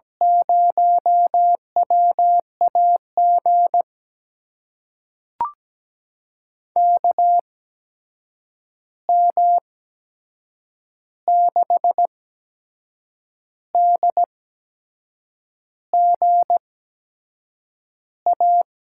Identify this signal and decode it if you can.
Morse code, heard as a keyed tone, about 17 wpm, spaced as for 6 wpm, 700 Hz. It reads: 0WAG KM6DGA
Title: USA Call Signs - 6 Characters - 8x Character Spacing - 17wpm